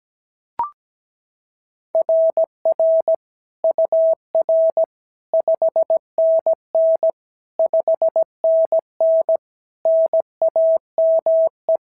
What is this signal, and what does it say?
Morse code, heard as a keyed tone, about 16 wpm, 650 Hz. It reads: RR UR 5NN 5NN NAME
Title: SOTA QSO Practice 17wpm